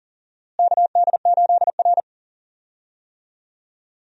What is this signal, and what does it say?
Morse code, heard as a keyed tone, about 39 wpm, 700 Hz. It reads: KD8R
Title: USA Call Signs - 4 Characters - Repeated 3x - 40wpm